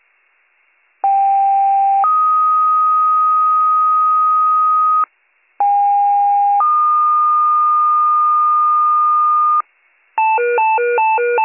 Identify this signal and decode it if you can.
Two-tone page, heard as a one-second tone, then a three-second tone, about 790 then 1250 Hz, about 810 then 1220 Hz. It is heard from 1.0 to 9.6 s.